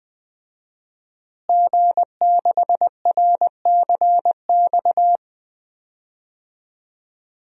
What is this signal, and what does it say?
Morse code, heard as a keyed tone, about 21 wpm, 700 Hz. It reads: G6RCX
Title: DXCC-100 Call Signs - 5 Characters - Repeated 3x - 20wpm